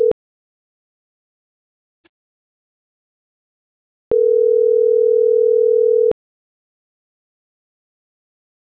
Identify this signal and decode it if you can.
Ringback tone (North America), 440 + 480 Hz.